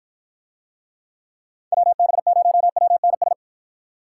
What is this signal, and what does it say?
Morse code, heard as a keyed tone, about 53 wpm, 700 Hz. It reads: WB0WNS